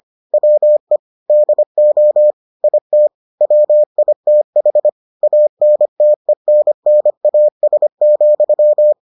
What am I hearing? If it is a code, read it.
Morse code, heard as a keyed tone, about 25 wpm, 600 Hz. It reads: WE DO IT WITH ANTENNAS,